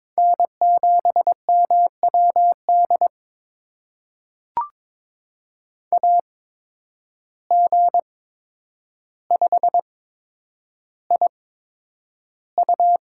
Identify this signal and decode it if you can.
Morse code, heard as a keyed tone, about 22 wpm, spaced as for 9 wpm, 700 Hz. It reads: N7MWD AG5IU
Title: USA Call Signs - 5 Characters - 8x Character Spacing - 22wpm